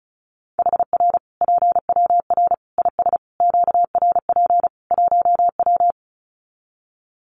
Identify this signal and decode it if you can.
Morse code, heard as a keyed tone, about 34 wpm, 700 Hz. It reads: HR PWR IS QRP 1W
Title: Common QSO Phrases 35wpm